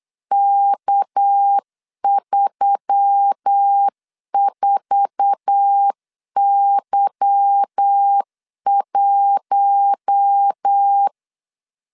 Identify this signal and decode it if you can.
Morse code, heard as a keyed tone, about 8 wpm, 800 Hz. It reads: K34Y1
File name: LEZIONI / 9 286 8wpm 3425